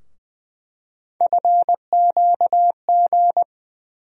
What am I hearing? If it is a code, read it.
Morse code, heard as a keyed tone, about 20 wpm, 700 Hz. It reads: FQG
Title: USA Call Sign Postfixes - 3 Letters 20wpm